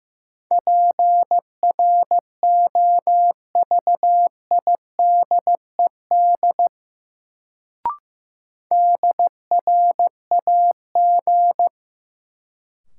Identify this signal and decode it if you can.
Morse code, heard as a keyed tone, about 15 wpm, 700 Hz. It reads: PROVIDED DRAG
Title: Sentences from Top 1500 Words - New Word Drill - 15wpm